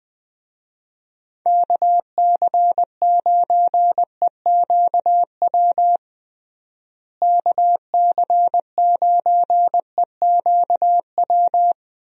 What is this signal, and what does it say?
Morse code, heard as a keyed tone, about 20 wpm, 700 Hz. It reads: KC9EQW KC9EQW
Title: USA Call Signs - 6 Characters - Repeated 2x - 20wpm